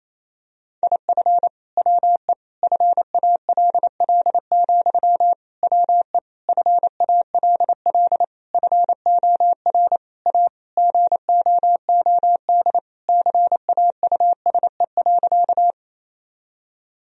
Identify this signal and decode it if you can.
Morse code, heard as a keyed tone, about 27 wpm, 700 Hz. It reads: IF WE FALL, WE FALL FOR A GOOD CAUSE.
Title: Sentences from Top 1000 Words - Review Part 2 - 28wpm